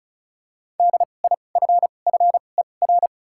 Morse code, heard as a keyed tone, about 34 wpm, spaced as for 25 wpm, 700 Hz. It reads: DIFFER
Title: 2x Character Spacing - Top 500 Words - 35wpm